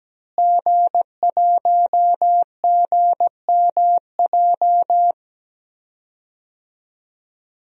Morse code, heard as a keyed tone, about 17 wpm, 700 Hz. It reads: G1GMJ
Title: DXCC-100 Call Signs - 5 Characters - Repeated 2x - 17wpm